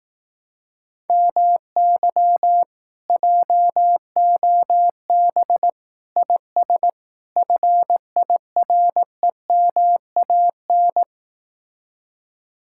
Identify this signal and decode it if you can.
Morse code, heard as a keyed tone, about 18 wpm, 700 Hz. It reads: MY JOB IS FIREMAN
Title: Speed-Racing Common QSO Phrases Encore - 15wpm